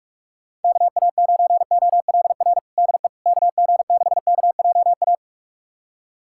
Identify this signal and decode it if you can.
Morse code, heard as a keyed tone, about 46 wpm, 690 Hz. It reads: KA9OLR DE KG6KJA
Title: USA Callsign DE Callsign - 6 Characters - 45wpm